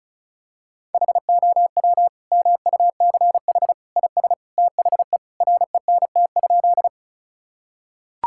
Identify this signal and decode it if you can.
Morse code, heard as a keyed tone, about 36 wpm, 700 Hz. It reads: HOW MUCH IS THE RENT?
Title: Sentences from Top 1000 Words - Review Part 5 - 35wpm